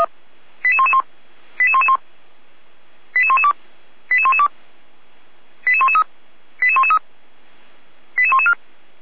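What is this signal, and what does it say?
Five-tone selective call (ZVEI), heard as quick runs of five tones, tones of about 70 ms. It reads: 88111 88111 88112 88112 88113 88113 88114
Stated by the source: ZVEI five-tone selective call received on SDR.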